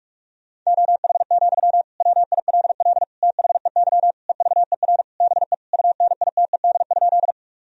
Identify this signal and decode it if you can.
Morse code, heard as a keyed tone, about 46 wpm, 700 Hz. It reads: OH, WILL THEY EVER BE UNITED?
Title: Sentences from Top 1000 Words - Review Part 4 - 45wpm